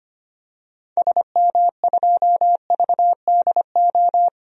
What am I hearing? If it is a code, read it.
Morse code, heard as a keyed tone, about 25 wpm, 700 Hz. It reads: SM2VDO
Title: DXCC-100 Call Signs - 6 Characters - 25wpm